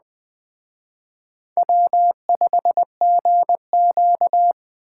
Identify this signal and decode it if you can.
Morse code, heard as a keyed tone, about 20 wpm, 700 Hz. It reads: W5GQ
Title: Speed Racing - USA 1x2 Call Signs 17wpm